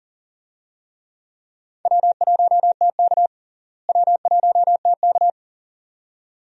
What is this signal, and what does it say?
Morse code, heard as a keyed tone, about 39 wpm, 700 Hz. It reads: W1TK W1TK
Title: USA Call Signs - 4 Characters - Repeated 2x - 40wpm